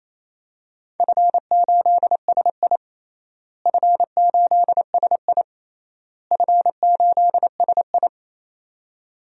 Morse code, heard as a keyed tone, about 28 wpm, 700 Hz. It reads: F8SI F8SI F8SI
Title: DXCC-100 Call Signs - 4 Characters - Repeated 3x - 28wpm